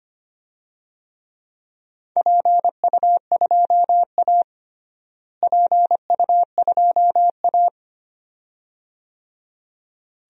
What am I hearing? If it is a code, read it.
Morse code, heard as a keyed tone, about 25 wpm, 700 Hz. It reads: PU2A PU2A
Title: DXCC-100 Call Signs - 4 Characters - Repeated 2x - 25wpm